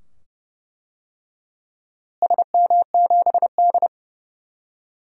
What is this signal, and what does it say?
Morse code, heard as a keyed tone, about 32 wpm, 700 Hz. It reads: SM7D
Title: DXCC-100 Call Signs - 4 Characters - Repeated 2x - 30wpm